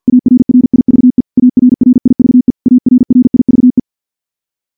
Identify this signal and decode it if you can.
Morse code, heard as a keyed tone, about 50 wpm, 250 Hz. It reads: ARRIVE ARRIVE ARRIVE
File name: morse / morse-training-50WPM